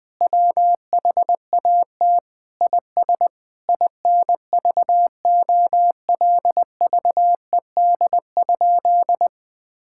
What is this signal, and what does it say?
Morse code, heard as a keyed tone, about 21 wpm, 700 Hz. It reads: WHAT IS INVOLVED?
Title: Sentences from Top 1000 Words - Review Part 5 - 20wpm